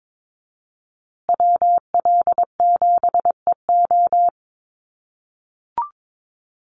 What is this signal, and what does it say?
Morse code, heard as a keyed tone, about 22 wpm, 700 Hz. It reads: WL7EO
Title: USA Call Signs Repeated 3x 22wpm